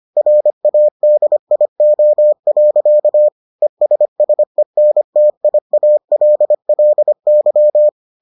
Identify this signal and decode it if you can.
Morse code, heard as a keyed tone, about 25 wpm, 600 Hz. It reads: RADIO. ESSENTIALLY